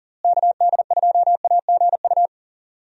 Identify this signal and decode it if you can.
Morse code, heard as a keyed tone, about 40 wpm, 700 Hz. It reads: KD2AGU
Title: USA Call Signs - 6 Characters - Repeated 2x - 40wpm